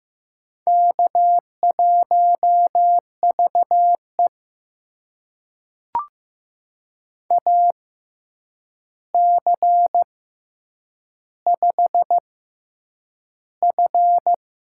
Morse code, heard as a keyed tone, about 15 wpm, spaced as for 8 wpm, 700 Hz. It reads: K1VE AC5F
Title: USA Call Signs - 4 Characters - 6x Character Spacing - 15wpm